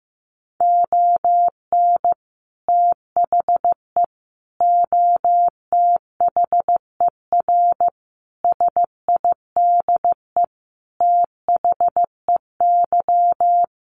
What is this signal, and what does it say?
Morse code, heard as a keyed tone, about 15 wpm, 700 Hz. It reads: ON THE OTHER SIDE THEY
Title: Sentences from Top 1200 Words 15wpm